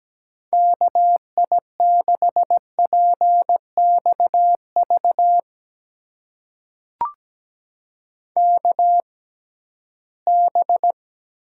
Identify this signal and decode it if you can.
Morse code, heard as a keyed tone, about 17 wpm, spaced as for 13 wpm, 700 Hz. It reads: KI6PXV KB
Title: USA Call Signs - 6 Characters - 6x Character Spacing - 17wpm